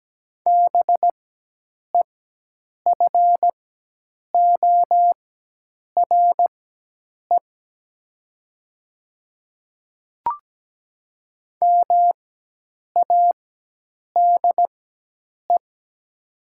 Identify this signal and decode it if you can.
Morse code, heard as a keyed tone, about 17 wpm, spaced as for 8 wpm, 700 Hz. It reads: BEFORE MADE
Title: 4x Character Spacing - Top 300 Words - 17wpm